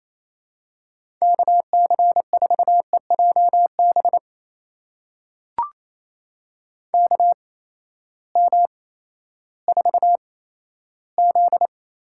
Morse code, heard as a keyed tone, about 28 wpm, 700 Hz. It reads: K C 4 E J B K M 4 Z
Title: USA Call Signs - 6 Characters - 8x Character Spacing - 28wpm